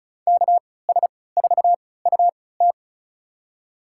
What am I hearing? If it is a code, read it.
Morse code, heard as a keyed tone, about 36 wpm, spaced as for 20 wpm, 700 Hz. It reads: KS4UT
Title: USA Call Signs - 5 Characters - 3x Character Spacing - 35wpm